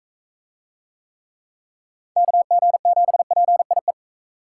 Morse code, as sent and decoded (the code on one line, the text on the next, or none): -.- --. --... .--. .. .
KG7PIE